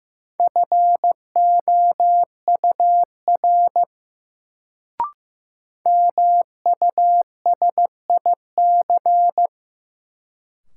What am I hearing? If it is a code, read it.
Morse code, heard as a keyed tone, about 15 wpm, 700 Hz. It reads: FOUR MUSIC
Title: Sentences from Top 300 Words - New Word Drill - 15wpm